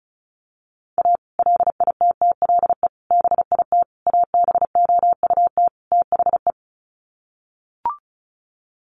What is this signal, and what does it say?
Morse code, heard as a keyed tone, about 34 wpm, 700 Hz. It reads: A LITTLE BIT ABOUT THE
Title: Sets of 5 Words Encore2 35wpm